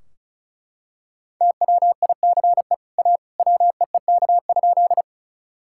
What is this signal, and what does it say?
Morse code, heard as a keyed tone, about 34 wpm, 700 Hz. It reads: TWICE A WEEK?
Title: Sentences from Top 1600 Words 35wpm